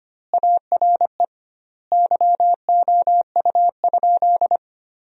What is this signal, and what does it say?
Morse code, heard as a keyed tone, about 25 wpm, 700 Hz. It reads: ARE YOU?
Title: 2x Word-Spacing - Sentences from Top 100 Words - 25wpm